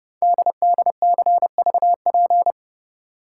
Morse code, heard as a keyed tone, about 30 wpm, 700 Hz. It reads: DDCVP
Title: Code Groups - 5 Letters - 30wpm